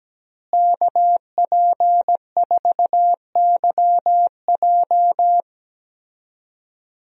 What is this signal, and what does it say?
Morse code, heard as a keyed tone, about 17 wpm, 700 Hz. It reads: KP4YJ
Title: DXCC-100 Call Signs - 5 Characters - Repeated 2x - 17wpm